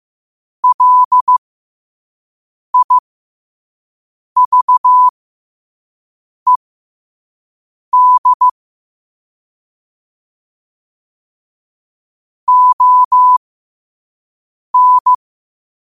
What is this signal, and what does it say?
Morse code, heard as a keyed tone, about 15 wpm, spaced as for 5 wpm, 1000 Hz. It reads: LIVED ON
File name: qotdfiles / 5wpm